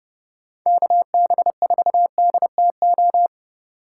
Morse code, heard as a keyed tone, about 29 wpm, 700 Hz. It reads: KB4DTO